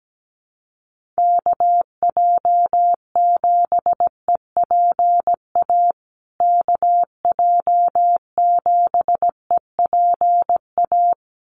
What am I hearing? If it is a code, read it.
Morse code, heard as a keyed tone, about 17 wpm, 700 Hz. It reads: KJ7EPA KJ7EPA
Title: USA Call Signs Repeated 2x 17wpm